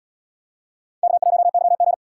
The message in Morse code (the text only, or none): K1YG